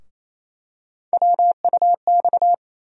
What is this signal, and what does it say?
Morse code, heard as a keyed tone, about 28 wpm, 700 Hz. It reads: WUX
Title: USA Call Sign Postfixes - 3 Letters 28wpm